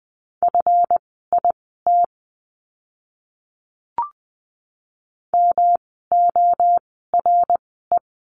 Morse code, heard as a keyed tone, about 20 wpm, spaced as for 14 wpm, 700 Hz. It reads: FIT MORE
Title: 139 Lesson 23f - Letter f - Words 20wpm